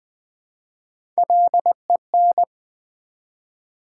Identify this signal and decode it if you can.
Morse code, heard as a keyed tone, about 20 wpm, 700 Hz. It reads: LEN
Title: Build A Word - 1 Letter At A Time - 3 to 5-Letter Words - 20wpm